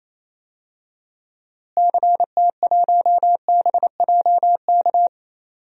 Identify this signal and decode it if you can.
Morse code, heard as a keyed tone, about 28 wpm, 700 Hz. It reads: CT1BJK
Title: DXCC-100 Call Signs - 6 Characters - Repeated 2x - 28wpm